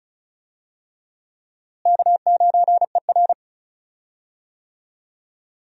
Morse code, heard as a keyed tone, about 36 wpm, 700 Hz. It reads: K9ER
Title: USA Call Signs - 4 Characters - Repeated 3x - 35wpm